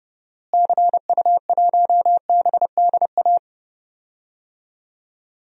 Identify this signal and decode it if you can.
Morse code, heard as a keyed tone, about 32 wpm, 700 Hz. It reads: CU1BDA
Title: DXCC-100 Call Signs - 6 Characters - Repeated 2x - 30wpm